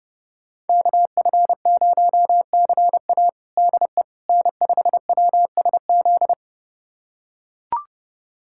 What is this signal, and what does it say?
Morse code, heard as a keyed tone, about 30 wpm, 700 Hz. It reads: KF0CA DE N5WSZ
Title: USA Callsign DE Callsign - 5 Characters - 30wpm